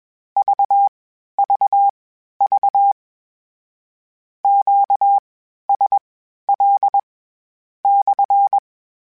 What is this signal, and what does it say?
Morse code, heard as a keyed tone, about 21 wpm, spaced as for 12 wpm, 800 Hz. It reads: VVV QSL /
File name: Q Calls / Q Calls-mp3s / Q CALLS SPSL @20